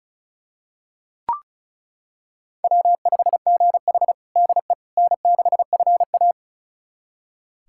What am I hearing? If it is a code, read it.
Morse code, heard as a keyed tone, about 36 wpm, 700 Hz. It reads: W5GH DE N6FA
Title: USA Callsign DE Callsign - 4 Characters - 35wpm